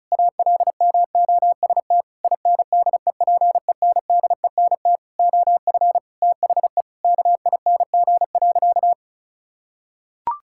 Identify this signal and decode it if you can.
Morse code, heard as a keyed tone, about 34 wpm, 700 Hz. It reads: ALMOST INDEPENDENT OF THE KING.
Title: Sentences from Top 1000 Words - Review Part 6 - 35wpm